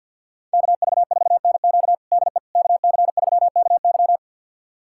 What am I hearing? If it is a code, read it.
Morse code, heard as a keyed tone, about 50 wpm, 700 Hz. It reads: KV4NQ DE KC3KY
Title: USA Callsign DE Callsign 50wpm